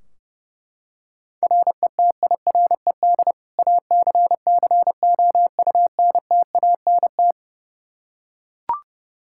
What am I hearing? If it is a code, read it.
Morse code, heard as a keyed tone, about 30 wpm, 700 Hz. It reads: RETIRED ACCOUNTANT
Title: K1USN SST QSO Practice 30wpm